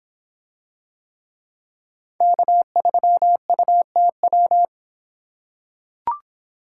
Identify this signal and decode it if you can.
Morse code, heard as a keyed tone, about 26 wpm, 700 Hz. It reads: K3UTW